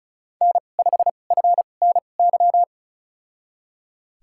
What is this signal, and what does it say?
Morse code, heard as a keyed tone, about 36 wpm, spaced as for 25 wpm, 700 Hz. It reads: N5FNY
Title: USA Call Signs - 5 Characters - 2x Character Spacing - 35wpm